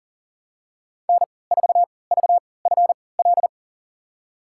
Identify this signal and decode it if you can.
Morse code, heard as a keyed tone, about 40 wpm, spaced as for 23 wpm, 700 Hz. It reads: N4VFL
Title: USA Call Signs - 3x Character Spacing - 40wpm